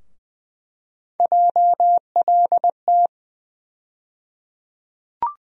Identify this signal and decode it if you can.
Morse code, heard as a keyed tone, about 20 wpm, 700 Hz. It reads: JLT
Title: USA Call Sign Postfixes - 3 Letters 20wpm